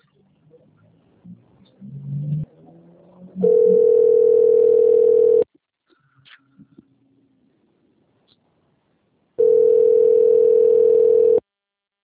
Ringback tone (North America), 440 + 480 Hz.